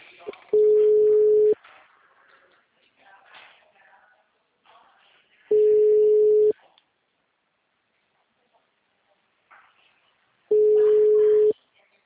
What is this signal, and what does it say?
Ringback tone (Europe, 425 Hz), with speech.